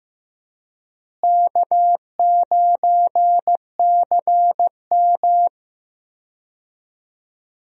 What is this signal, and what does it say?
Morse code, heard as a keyed tone, about 16 wpm, 700 Hz. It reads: K9CM
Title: USA Call Signs - 4 Characters - Repeated 2x - 15wpm